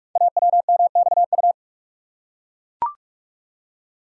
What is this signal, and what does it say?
Morse code, heard as a keyed tone, about 44 wpm, 700 Hz. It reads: AWMXU
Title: Code Groups - 5 Letters - 45wpm